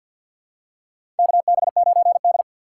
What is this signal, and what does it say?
Morse code, heard as a keyed tone, about 47 wpm, 700 Hz. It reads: KB9D
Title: USA Call Signs - 4 Characters - 2x Character Spacing - 50wpm